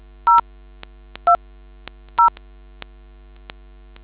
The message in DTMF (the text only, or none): 0 2 0